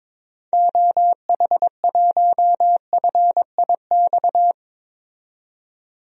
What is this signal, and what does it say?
Morse code, heard as a keyed tone, about 21 wpm, 700 Hz. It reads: OH1FIX
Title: DXCC-100 Call Signs - 6 Characters - 22wpm